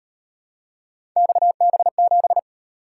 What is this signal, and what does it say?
Morse code, heard as a keyed tone, about 36 wpm, 700 Hz. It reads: XB7